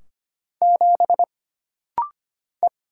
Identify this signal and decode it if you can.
Morse code, heard as a keyed tone, about 25 wpm, 700 Hz. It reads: 7 E
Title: Single Letter-Number Alt - Rapid-Fire 25wpm